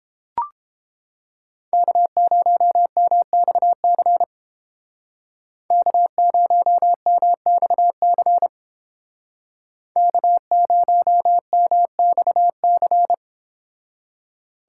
Morse code, heard as a keyed tone, about 30 wpm, 700 Hz. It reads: K0MXC K0MXC K0MXC